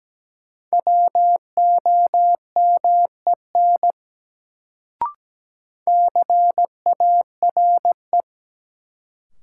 Morse code, heard as a keyed tone, about 17 wpm, 700 Hz. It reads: WOMEN CARE
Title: Sets of 3 Words - New Word Drill - 17wpm